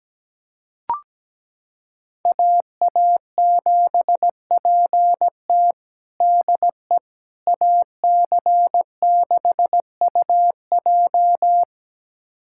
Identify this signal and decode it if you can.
Morse code, heard as a keyed tone, about 17 wpm, 700 Hz. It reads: AA7PT DE AC6UJ
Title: USA Callsign DE Callsign - 5 Characters - 17wpm